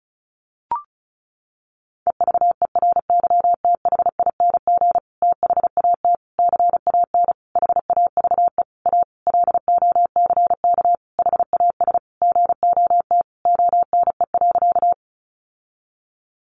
Morse code, heard as a keyed tone, about 34 wpm, 700 Hz. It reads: EVERYTHING THAT CAN HAVE A LOCK HAS GOT ONE.
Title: Sentences from Top 1300 Words 35wpm